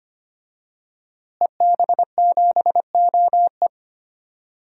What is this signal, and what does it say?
Morse code, heard as a keyed tone, about 25 wpm, 700 Hz. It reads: EB7OE
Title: DXCC-100 Call Signs - 5 Characters - Repeated 2x - 25wpm